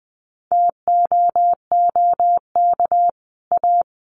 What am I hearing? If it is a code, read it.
Morse code, heard as a keyed tone, about 20 wpm, 700 Hz. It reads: TOOK A
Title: Sets of 5 Words Encore 20wpm